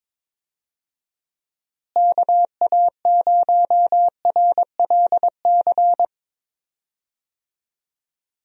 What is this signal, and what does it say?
Morse code, heard as a keyed tone, about 22 wpm, 700 Hz. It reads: KA0RLC
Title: USA Call Signs - 6 Characters - Repeated 3x - 22wpm